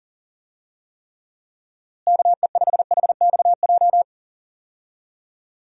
Morse code, heard as a keyed tone, about 40 wpm, 700 Hz. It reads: KE5HXJ
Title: USA Call Signs - 6 Characters - Repeated 2x - 40wpm